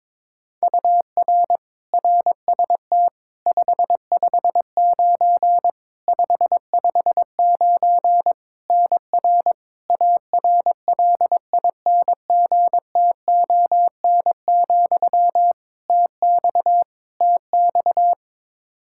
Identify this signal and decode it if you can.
Morse code, heard as a keyed tone, about 21 wpm, 700 Hz. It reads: UR RST 559 559 NR ARLINGTON, TX TX
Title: Practice QSO - Exchange 1 - Beginning - 22wpm